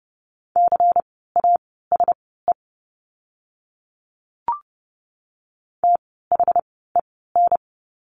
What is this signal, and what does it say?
Morse code, heard as a keyed tone, about 30 wpm, spaced as for 17 wpm, 700 Hz. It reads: CASE THEN
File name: Lesson 16f - Letter c - Words 30wpm